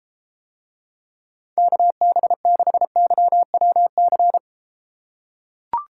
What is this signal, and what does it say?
Morse code, heard as a keyed tone, about 34 wpm, 700 Hz. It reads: KB6YWC